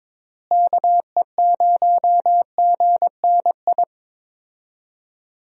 Morse code, heard as a keyed tone, about 21 wpm, 700 Hz. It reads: KE0GNI
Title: USA Call Signs - 6 Characters - Repeated 2x - 22wpm